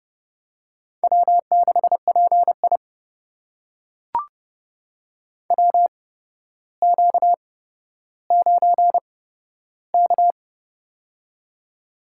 Morse code, heard as a keyed tone, about 30 wpm, spaced as for 13 wpm, 700 Hz. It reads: W6PI WQ9K
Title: USA Call Signs - 4 Characters - 8x Character Spacing - 30wpm